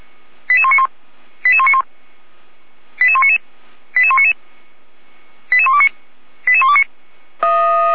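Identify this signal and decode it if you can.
Five-tone selective call (ZVEI), heard as quick runs of five tones, tones of about 70 ms. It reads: 88191 88191 88199 88199 88129 88129
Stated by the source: Five-tone ZVEI selective call received by SDR, logged on server of Feuerwehr Orlamünde (fire brigade).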